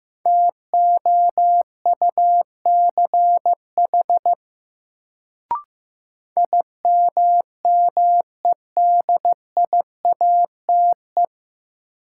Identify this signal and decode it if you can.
Morse code, heard as a keyed tone, about 15 wpm, 700 Hz. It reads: TOUCH IMMEDIATE
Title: Sentences from Top 1200 Words - New Word Drill - 15wpm